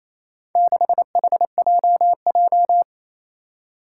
Morse code, heard as a keyed tone, about 27 wpm, 700 Hz. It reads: 6HJJ
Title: Four Challenging Letter-Number Combinations 28wpm